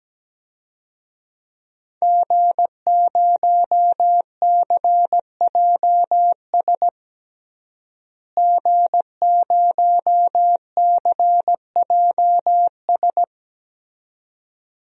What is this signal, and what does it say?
Morse code, heard as a keyed tone, about 17 wpm, 700 Hz. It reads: G0CJS G0CJS
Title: DXCC-100 Call Signs - 5 Characters - Repeated 2x - 17wpm